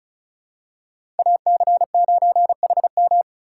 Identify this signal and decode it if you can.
Morse code, heard as a keyed tone, about 36 wpm, 700 Hz. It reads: AC9HM
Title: USA Call Signs - 5 Characters - 3x Character Spacing - 35wpm